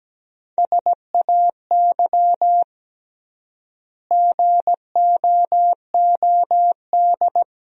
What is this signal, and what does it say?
Morse code, heard as a keyed tone, about 17 wpm, 700 Hz. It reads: SAY GOOD
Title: 3x Word-Spacing - Sentences from Top 300 Words - 17wpm